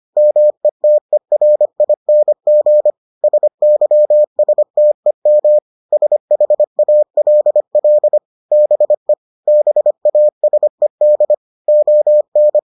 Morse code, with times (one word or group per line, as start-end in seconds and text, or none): 0.2-2.9 METERING
3.2-5.6 SYSTEM
5.9-8.2 SHALL
8.5-9.1 BE
9.5-11.3 BASED
11.7-12.6 ON